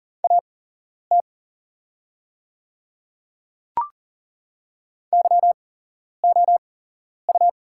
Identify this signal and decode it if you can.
Morse code, heard as a keyed tone, about 41 wpm, spaced as for 11 wpm, 700 Hz. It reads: AT YOU
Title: 8x Character Spacing - Top 100 Words - 40wpm